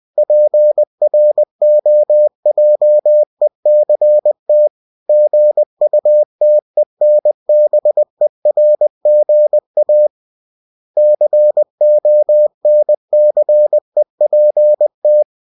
Morse code, heard as a keyed tone, about 20 wpm, 600 Hz. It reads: PROJECT GUTENBERGA CONCEPT
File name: cw-books / winnie-the-pooh / winnie-20wpm-page-0122